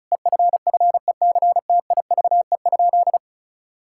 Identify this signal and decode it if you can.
Morse code, heard as a keyed tone, about 36 wpm, 700 Hz. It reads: EFFECTIVE?
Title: Sentences from Top 1000 Words - Review Part 6 - 35wpm